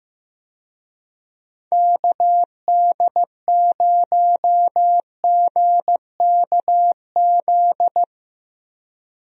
Morse code, heard as a keyed tone, about 15 wpm, 700 Hz. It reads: KD0GKZ
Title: USA Call Signs - 6 Characters - Repeated 3x - 15wpm